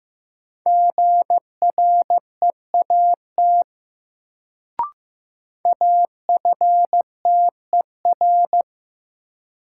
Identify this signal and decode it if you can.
Morse code, heard as a keyed tone, about 16 wpm, 700 Hz. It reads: GREAT AFTER
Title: Sets of 2 Words - New Word Drill - 15wpm